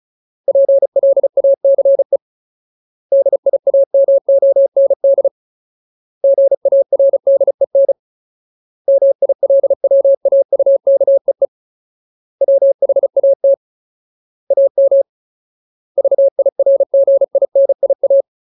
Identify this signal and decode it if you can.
Morse code, heard as a keyed tone, about 34 wpm, 550 Hz. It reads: PLACE DIAMOND GARDEN MILWAUKEE WHAT AM VIRGINIA
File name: Ultimate Word List / 35WPM Ultimate